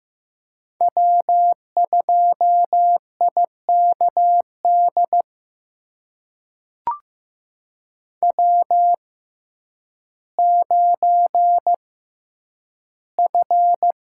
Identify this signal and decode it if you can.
Morse code, heard as a keyed tone, about 15 wpm, spaced as for 9 wpm, 700 Hz. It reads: W2IKD W9F
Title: USA Call Signs - 5 Characters - 6x Character Spacing - 15wpm